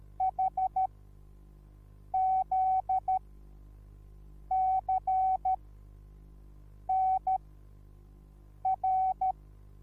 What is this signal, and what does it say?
Morse code, heard as a keyed tone, about 13 wpm, spaced as for 5 wpm, 760 Hz. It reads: HZCNR
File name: Onlinelehrgang / morsen / DJ4UF Morsen21